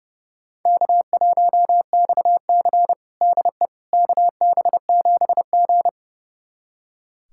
Morse code, heard as a keyed tone, about 30 wpm, 700 Hz. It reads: K1XC DE KB7G
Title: USA Callsign DE Callsign - 4 Characters - 30wpm